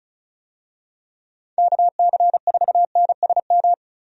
Morse code, heard as a keyed tone, about 34 wpm, 700 Hz. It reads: KC4NSM
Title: USA Call Signs - 6 Characters - 35wpm